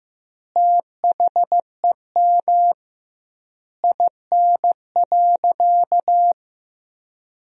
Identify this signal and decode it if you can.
Morse code, heard as a keyed tone, about 15 wpm, 700 Hz. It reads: THEM IN.
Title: 2x Word-Spacing - Sentences from Top 500 Words - 15wpm